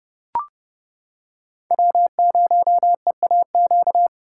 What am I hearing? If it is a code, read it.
Morse code, heard as a keyed tone, about 29 wpm, 700 Hz. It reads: W0EAQ